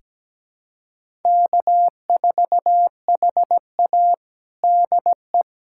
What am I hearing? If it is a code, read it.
Morse code, heard as a keyed tone, about 17 wpm, 700 Hz. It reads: K4HA DE
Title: USA Callsign DE Callsign - 4 Characters - 17wpm